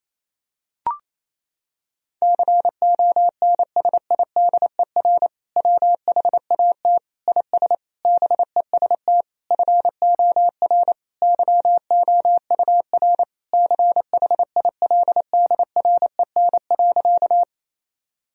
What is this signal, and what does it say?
Morse code, heard as a keyed tone, about 27 wpm, 700 Hz. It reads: CONSIDER WHAT IS BEST FOR YOUR CHILDREN.
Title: Sentences from Top 1000 Words - Review Part 4 - 28wpm